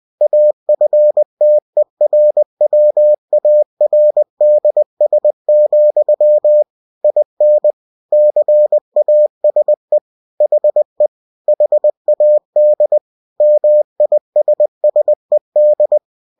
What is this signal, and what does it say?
Morse code, heard as a keyed tone, about 20 wpm, 600 Hz. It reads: AFTERWARDS, IN CASE HE HAD MISSED
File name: cw-books / winnie-the-pooh / winnie-20wpm-page-0029